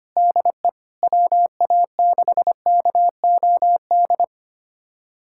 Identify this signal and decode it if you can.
Morse code, heard as a keyed tone, about 25 wpm, 700 Hz. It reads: DE WA6KOD
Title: USA Callsign DE Callsign - 6 Characters - 25wpm